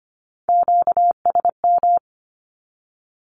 Morse code, heard as a keyed tone, about 25 wpm, 700 Hz. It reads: QSM
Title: Q-Codes with Spoken Meaning - 25wpm